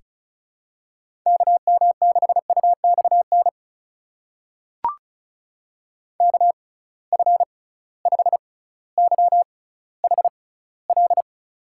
Morse code, heard as a keyed tone, about 36 wpm, spaced as for 18 wpm, 700 Hz. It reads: KM6UXN KF5YHL
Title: USA Call Signs - 6 Characters - 6x Character Spacing - 35wpm